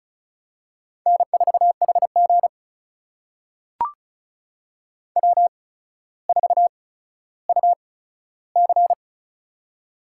Morse code, heard as a keyed tone, about 36 wpm, spaced as for 15 wpm, 700 Hz. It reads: N4HG W4UC
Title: USA Call Signs - 4 Characters - 8x Character Spacing - 35wpm